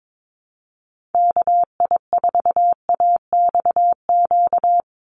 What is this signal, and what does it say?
Morse code, heard as a keyed tone, about 22 wpm, 700 Hz. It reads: KI4AXQ
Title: USA Call Signs 22wpm